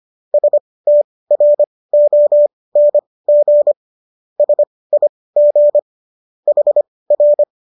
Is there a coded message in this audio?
Yes: Morse code from 0.3 to 7.4 s, heard as a keyed tone, about 25 wpm, spaced as for 18 wpm, 590 Hz. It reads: STRONG SIG HR